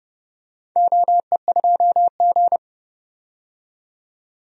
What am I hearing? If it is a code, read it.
Morse code, heard as a keyed tone, about 30 wpm, 700 Hz. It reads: OE2G
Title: DXCC-100 Call Signs - 4 Characters - 30wpm